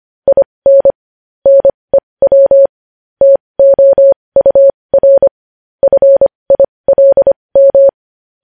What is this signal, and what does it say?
Morse code, heard as a keyed tone, about 25 wpm, spaced as for 20 wpm, 550 Hz. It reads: IN NEW TOUR FILM
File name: podcast / News headlines-1x-25x20